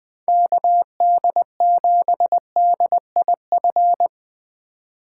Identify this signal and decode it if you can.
Morse code, heard as a keyed tone, about 21 wpm, 700 Hz. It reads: KD7DIF